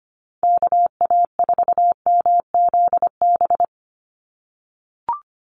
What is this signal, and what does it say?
Morse code, heard as a keyed tone, about 25 wpm, 700 Hz. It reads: KA4MZB
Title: USA Call Signs 25wpm